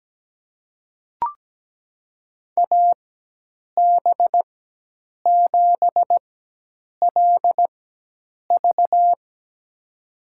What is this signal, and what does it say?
Morse code, heard as a keyed tone, about 17 wpm, spaced as for 8 wpm, 700 Hz. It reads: AB7LV
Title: USA Call Signs - 4x Character Spacing - 17wpm